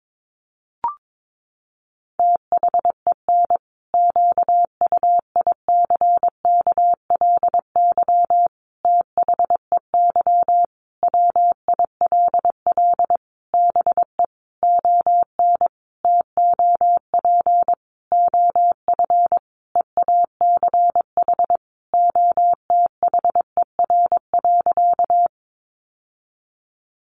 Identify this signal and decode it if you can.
Morse code, heard as a keyed tone, about 22 wpm, 700 Hz. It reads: THEN QUICKLY THEY WILL BE ON TOP OF EACH OTHER.
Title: Sentences from Top 700 Words 22wpm